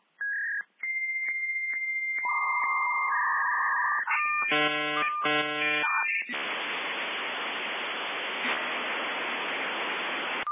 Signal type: dial-up modem handshake